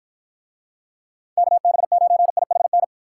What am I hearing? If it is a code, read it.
Morse code, heard as a keyed tone, about 54 wpm, 700 Hz. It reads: KB9IHN